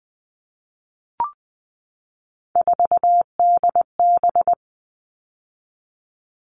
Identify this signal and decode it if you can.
Morse code, heard as a keyed tone, about 20 wpm, 700 Hz. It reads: 4DB
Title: All Cut Numbers - 3-Digit 20wpm